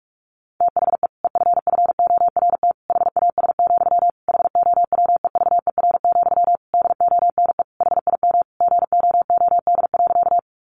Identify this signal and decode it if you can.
Morse code, heard as a keyed tone, about 44 wpm, 700 Hz. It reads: THE EFFORT HAS, HOWEVER, DONE HIM GOOD.
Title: Sentences from Top 1200 Words 45wpm